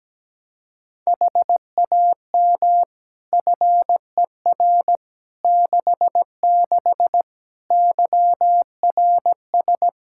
Morse code, heard as a keyed tone, about 17 wpm, 700 Hz. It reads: HAM FER 66 YRS
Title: K1USN SST QSO Practice 17wpm